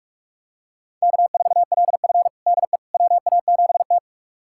Morse code, heard as a keyed tone, about 42 wpm, 700 Hz. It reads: K4LF DE WA7T
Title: USA Callsign DE Callsign - 4 Characters - 45wpm